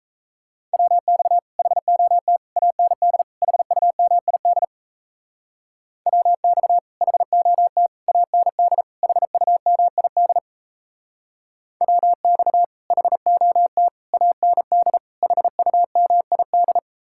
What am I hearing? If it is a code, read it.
Morse code, heard as a keyed tone, about 34 wpm, 700 Hz. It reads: WX HOT AND HUMID WX HOT AND HUMID WX HOT AND HUMID
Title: Speed-Racing Common QSO Phrases - 28wpm